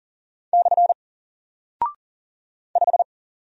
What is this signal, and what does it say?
Morse code, heard as a keyed tone, about 40 wpm, 700 Hz. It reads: / 5
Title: Single Character (Letter:Number:Punctuation:Prosign) 40wpm